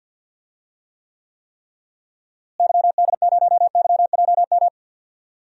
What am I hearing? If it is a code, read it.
Morse code, heard as a keyed tone, about 50 wpm, 700 Hz. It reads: YD0YJM
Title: DXCC-100 Call Signs - 6 Characters - Repeated 3x - 50wpm